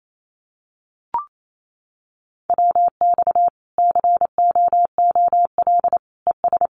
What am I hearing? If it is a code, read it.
Morse code, heard as a keyed tone, about 28 wpm, 700 Hz. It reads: WX COOL ES
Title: Common QSO Phrases 28wpm